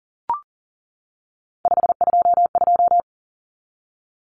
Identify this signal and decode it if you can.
Morse code, heard as a keyed tone, about 37 wpm, 700 Hz. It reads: 522
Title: Common Cut Numbers - 3-Digit 40wpm